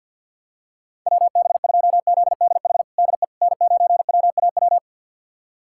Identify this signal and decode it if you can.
Morse code, heard as a keyed tone, about 46 wpm, 700 Hz. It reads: WB2ZDH DE N9WAW
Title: USA Callsign DE Callsign 50wpm